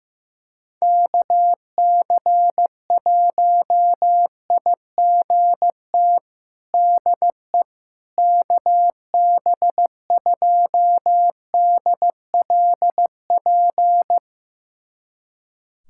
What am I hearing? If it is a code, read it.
Morse code, heard as a keyed tone, about 15 wpm, 700 Hz. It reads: KC1IGT DE KB2DLP
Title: USA Callsign DE Callsign - 6 Characters - 15wpm